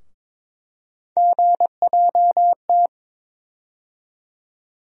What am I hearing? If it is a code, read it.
Morse code, heard as a keyed tone, about 21 wpm, 700 Hz. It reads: GJT
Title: USA Call Sign Postfixes - 3 Letters 22wpm